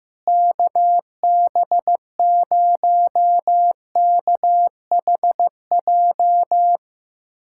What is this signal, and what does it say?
Morse code, heard as a keyed tone, about 15 wpm, 700 Hz. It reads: KB0KHJ
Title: USA Callsign DE Callsign - 6 Characters - 15wpm